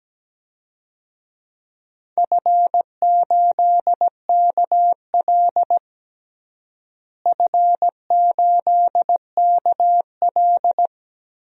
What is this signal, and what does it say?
Morse code, heard as a keyed tone, about 17 wpm, 700 Hz. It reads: F8KL F8KL
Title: DXCC-100 Call Signs - 4 Characters - Repeated 3x - 17wpm